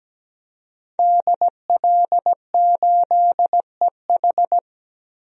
Morse code, heard as a keyed tone, about 17 wpm, 700 Hz. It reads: DL8EH
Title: DXCC-100 Call Signs - 5 Characters - Repeated 2x - 17wpm